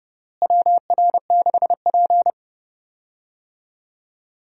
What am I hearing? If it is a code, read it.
Morse code, heard as a keyed tone, about 32 wpm, 700 Hz. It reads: WR6P
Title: USA Call Signs - 4 Characters - Repeated 2x - 30wpm